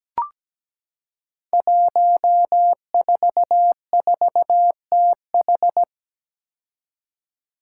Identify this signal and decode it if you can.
Morse code, heard as a keyed tone, about 17 wpm, 700 Hz. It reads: 144TH
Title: Five Challenging Letter-Number Combinations 17wpm